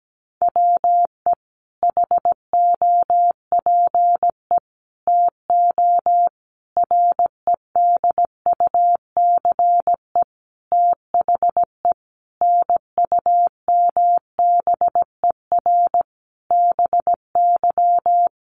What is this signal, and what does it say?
Morse code, heard as a keyed tone, about 17 wpm, 700 Hz. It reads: WE HOPE TO REDUCE THE NUMBER BY
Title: Sentences from Top 1100 Words 17wpm